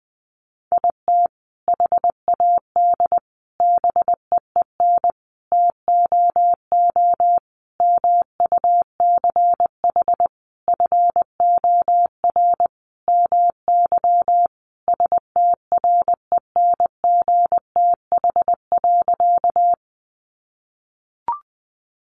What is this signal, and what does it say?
Morse code, heard as a keyed tone, about 19 wpm, 700 Hz. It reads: IT HAD BEEN TOO MUCH FOR MY STRENGTH.
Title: Sentences from Top 1300 Words 20wpm